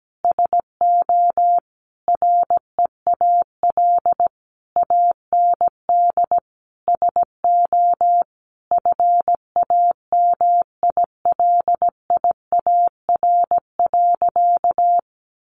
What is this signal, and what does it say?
Morse code, heard as a keyed tone, about 17 wpm, 700 Hz. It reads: SO REAL AND SO FAMILIAR.
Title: Sentences from Top 1000 Words 17wpm